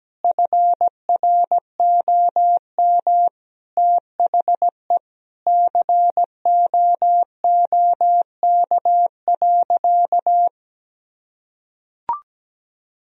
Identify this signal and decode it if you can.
Morse code, heard as a keyed tone, about 17 wpm, 700 Hz. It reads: FROM THE COOK.
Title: Sentences from Top 1000 Words - Review Part 4 - 17wpm